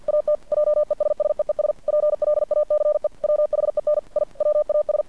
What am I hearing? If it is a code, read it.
Morse code, heard as a keyed tone, about 50 wpm, 600 Hz. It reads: AT JESSEES PLACE WHEN I WAS
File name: archived / CRSnet